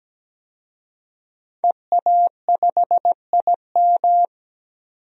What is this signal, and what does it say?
Morse code, heard as a keyed tone, about 17 wpm, 700 Hz. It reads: EA5IM
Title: DXCC-100 Call Signs - 5 Characters - 17wpm